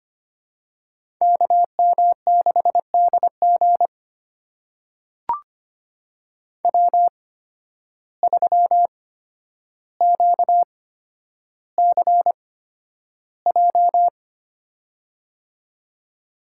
Morse code, heard as a keyed tone, about 25 wpm, spaced as for 11 wpm, 700 Hz. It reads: KM6DG W3QCJ
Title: USA Call Signs - 8x Character Spacing - 25wpm